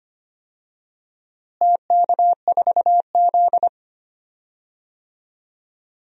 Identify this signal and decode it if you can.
Morse code, heard as a keyed tone, about 25 wpm, 700 Hz. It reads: TK4Z
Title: DXCC-100 Call Signs - 4 Characters - Repeated 2x - 25wpm